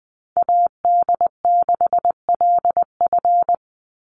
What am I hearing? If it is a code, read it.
Morse code, heard as a keyed tone, about 20 wpm, 700 Hz. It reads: AD6LF
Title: USA Call Signs 20wpm